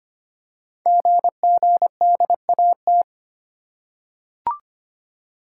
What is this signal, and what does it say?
Morse code, heard as a keyed tone, about 25 wpm, 700 Hz. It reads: GGDAT
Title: Code Groups - 5 Letters - 25wpm